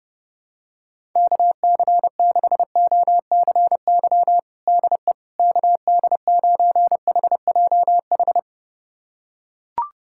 Morse code, heard as a keyed tone, about 32 wpm, 700 Hz. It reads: KC6OCY DE KD9HJH
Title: USA Callsign DE Callsign - 6 Characters - 30wpm